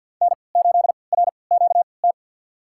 Morse code, heard as a keyed tone, about 52 wpm, spaced as for 29 wpm, 700 Hz. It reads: N8RQT
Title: USA Call Signs - 5 Characters - 3x Character Spacing - 50wpm